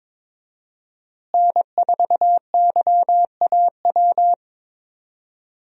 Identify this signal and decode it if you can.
Morse code, heard as a keyed tone, about 22 wpm, 700 Hz. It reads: N4YAW
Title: USA Call Signs - 5 Characters - Repeated 3x - 22wpm